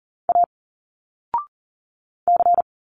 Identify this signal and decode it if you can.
Morse code, heard as a keyed tone, about 40 wpm, 700 Hz. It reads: A C
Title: Single Letters 40wpm